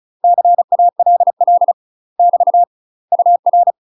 Morse code, heard as a keyed tone, about 34 wpm, 700 Hz. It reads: CALL <BT> UR
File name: UK MASTER QSOs TRAINING / 35WPM / UK MASTER QSOs 0004